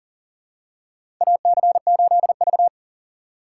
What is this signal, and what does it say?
Morse code, heard as a keyed tone, about 37 wpm, 700 Hz. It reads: AC8V
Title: USA Call Signs - 4 Characters - 3x Character Spacing - 40wpm